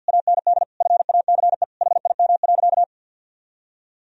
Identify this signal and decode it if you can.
Morse code, heard as a keyed tone, about 50 wpm, 700 Hz. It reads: AND FACE HIM.
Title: Sentences from Top 1000 Words - Review Part 2 - 50wpm